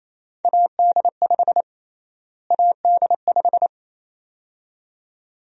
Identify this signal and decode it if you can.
Morse code, heard as a keyed tone, about 28 wpm, 700 Hz. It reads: AD5 AD5
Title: USA Call Sign Prefixes - Repeated 2x - 28wpm